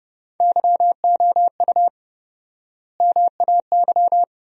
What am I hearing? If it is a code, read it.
Morse code, heard as a keyed tone, about 32 wpm, 700 Hz. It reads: YOU MAY
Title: 4x Word-Spacing - Sentences from Top 100 Words - 30wpm